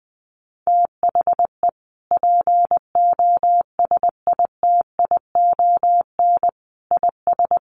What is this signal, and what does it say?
Morse code, heard as a keyed tone, about 20 wpm, 700 Hz. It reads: THE POSITION IS
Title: Sentences from Top 1300 Words 20wpm